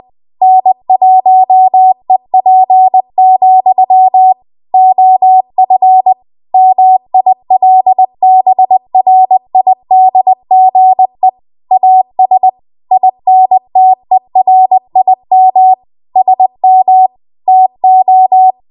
Morse code, heard as a keyed tone, about 20 wpm, 750 Hz. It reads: N1EP, OF MILBRIDGE AS INTERIM SM TO
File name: Archive / 20 WPM / 20WPM